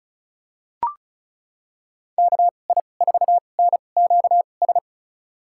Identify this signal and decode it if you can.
Morse code, heard as a keyed tone, about 36 wpm, spaced as for 25 wpm, 700 Hz. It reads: KI4NQS
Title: USA Call Signs - 6 Characters - 2x Character Spacing - 35wpm